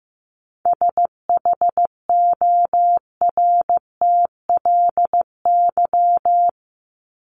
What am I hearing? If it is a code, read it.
Morse code, heard as a keyed tone, about 15 wpm, 700 Hz. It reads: SHORTLY
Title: LY Suffix - Top 100 Words with Base Words 15wpm